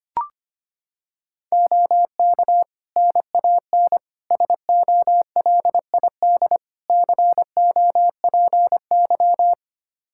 Morse code, heard as a keyed tone, about 25 wpm, 700 Hz. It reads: OK NAN SOLID COPY
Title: Practice QSO - Exchange 3 - More Info - 25wpm